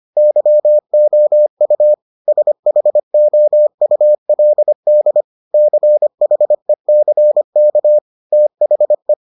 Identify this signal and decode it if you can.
Morse code, heard as a keyed tone, about 25 wpm, 600 Hz. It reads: YOU SHOULD CHECK THE